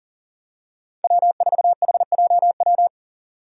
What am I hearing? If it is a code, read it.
Morse code, heard as a keyed tone, about 39 wpm, 700 Hz. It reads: W4HJW
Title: Five Challenging Letter-Number Combinations 40wpm